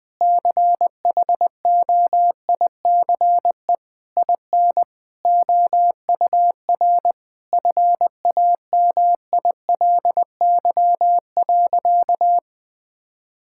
Morse code, heard as a keyed tone, about 20 wpm, 700 Hz. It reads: CHOICE IN OUR FAMILY.
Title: Sentences from Top 1000 Words - Review Part 6 - 20wpm